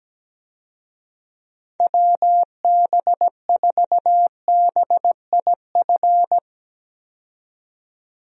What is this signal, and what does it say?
Morse code, heard as a keyed tone, about 17 wpm, 700 Hz. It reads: WB4BIF
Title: USA Call Signs - 6 Characters - 17wpm